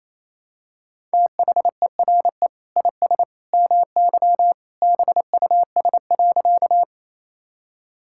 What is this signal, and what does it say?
Morse code, heard as a keyed tone, about 28 wpm, 700 Hz. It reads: THERE IS MY BUS.
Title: Sentences from Top 1000 Words - Review Part 6 - 28wpm